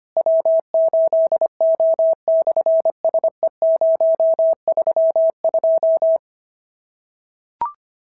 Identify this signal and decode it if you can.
Morse code, heard as a keyed tone, about 25 wpm, 650 Hz. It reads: W8O/SE032
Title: SOTA QSO Practice 25wpm